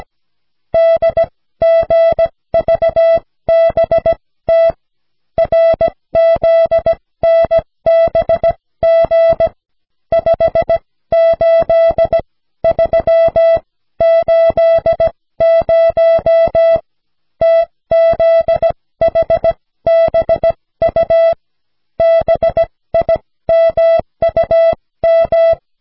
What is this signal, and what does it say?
Morse code, heard as a keyed tone, about 16 wpm, spaced as for 14 wpm, 660 Hz. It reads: DGVBT RZNBG 58380 TZHBU BIMUM
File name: cw / snelheid 4